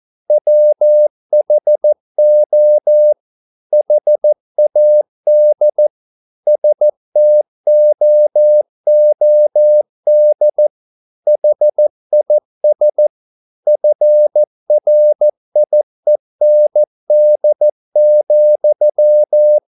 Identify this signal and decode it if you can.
Morse code, heard as a keyed tone, about 14 wpm, 600 Hz. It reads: WHO HAD STOOD HIS FRIEND,
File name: mp3 files / a dogs mistake - 14wpm